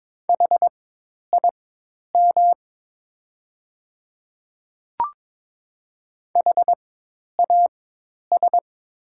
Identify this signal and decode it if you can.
Morse code, heard as a keyed tone, about 22 wpm, spaced as for 10 wpm, 700 Hz. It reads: HIM HAS
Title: 4x Character Spacing - Top 100 Words - 22wpm